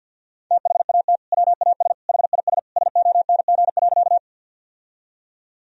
Morse code, heard as a keyed tone, about 48 wpm, 700 Hz. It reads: THAT WAS HIS SONG.
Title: Sentences from Top 1000 Words - Review Part 2 - 50wpm